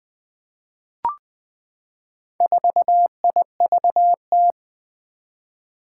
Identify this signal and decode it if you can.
Morse code, heard as a keyed tone, about 20 wpm, 700 Hz. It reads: 4IVT
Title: Four Challenging Letter-Number Combinations 20wpm